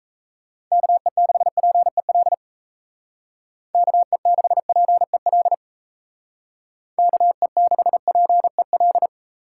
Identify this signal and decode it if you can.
Morse code, heard as a keyed tone, about 34 wpm, 700 Hz. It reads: KE6PEL KE6PEL KE6PEL
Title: Speed Racing - USA 2x3 Call Signs 28wpm